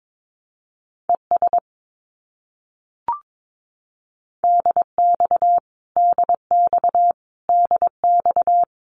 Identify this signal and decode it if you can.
Morse code, heard as a keyed tone, about 22 wpm, 700 Hz. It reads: ES DX DX DX
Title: Abbreviations Repeated 3x - 22wpm